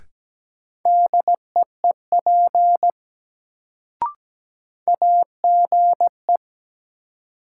Morse code, heard as a keyed tone, about 17 wpm, 700 Hz. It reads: DEEP AGE
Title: Sentences from Top 500 Words - New Word Drill - 17wpm